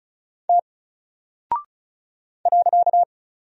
Morse code, heard as a keyed tone, about 36 wpm, 700 Hz. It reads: T .
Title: Single Character (Letter:Number:Punctuation:Prosign) Alt 35wpm